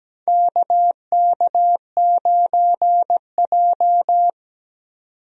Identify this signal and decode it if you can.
Morse code, heard as a keyed tone, about 17 wpm, 700 Hz. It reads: KK9J
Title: USA Call Signs - 4 Characters - 17wpm